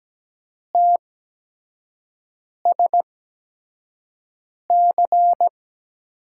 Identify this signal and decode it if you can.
Morse code, heard as a keyed tone, about 17 wpm, spaced as for 5 wpm, 700 Hz. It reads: TSC
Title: USA Call Signs - 6 Characters - 8x Character Spacing - 17wpm